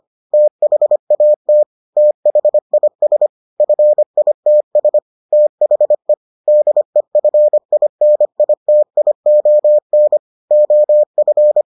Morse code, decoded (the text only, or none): THAT THIS FITS THE DEFINITION OF